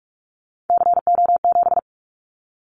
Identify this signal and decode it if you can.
Morse code, heard as a keyed tone, about 44 wpm, 700 Hz. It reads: CO7
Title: Callsign Prefixes for Top 100 DXCC - 45wpm